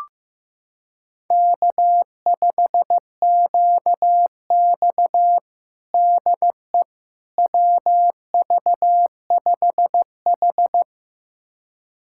Morse code, heard as a keyed tone, about 15 wpm, 700 Hz. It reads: K5QX DE WV5H
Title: USA Callsign DE Callsign - 4 Characters - 15wpm